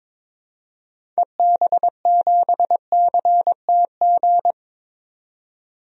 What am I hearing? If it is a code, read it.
Morse code, heard as a keyed tone, about 22 wpm, 700 Hz. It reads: EB7CTG
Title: DXCC-100 Call Signs - 6 Characters - 22wpm